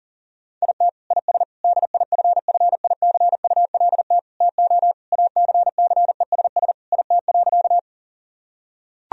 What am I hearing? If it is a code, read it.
Morse code, heard as a keyed tone, about 40 wpm, 700 Hz. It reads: IT IS DIFFICULT TO ACCESS IT.
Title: Sentences from Top 1000 Words - Review Part 4 - 40wpm